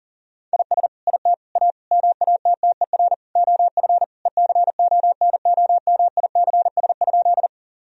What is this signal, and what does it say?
Morse code, heard as a keyed tone, about 40 wpm, 700 Hz. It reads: IS IT A MATTER OF ECONOMICS?
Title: Sentences from Top 1000 Words - Review Part 3 - 40wpm